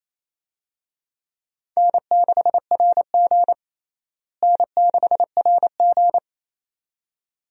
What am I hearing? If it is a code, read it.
Morse code, heard as a keyed tone, about 28 wpm, 700 Hz. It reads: N6RG N6RG
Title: USA Call Signs - 4 Characters - Repeated 2x - 28wpm